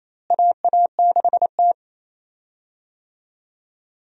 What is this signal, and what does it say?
Morse code, heard as a keyed tone, about 28 wpm, 700 Hz. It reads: AA6T
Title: USA Call Signs - 4 Characters - Repeated 2x - 28wpm